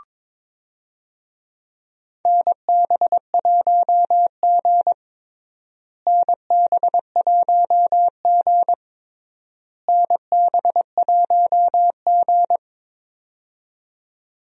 Morse code, heard as a keyed tone, about 21 wpm, 700 Hz. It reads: NB1G NB1G NB1G
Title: USA Call Signs - 4 Characters - Repeated 3x - 22wpm